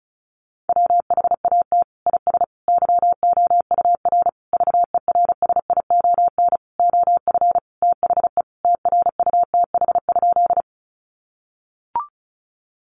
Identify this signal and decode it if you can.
Morse code, heard as a keyed tone, about 34 wpm, 700 Hz. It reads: WHAT IS YOUR VERSION OF THE TRUTH?
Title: Sentences from Top 1400 Words 35wpm